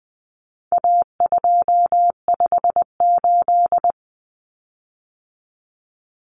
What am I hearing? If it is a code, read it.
Morse code, heard as a keyed tone, about 20 wpm, 700 Hz. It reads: A258
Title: Common Cut Numbers - 4-Digit 20wpm